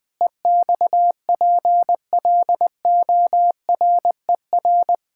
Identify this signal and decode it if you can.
Morse code, heard as a keyed tone, about 20 wpm, 700 Hz. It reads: EXPLORER
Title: Pangram Challenge v2 - 20wpm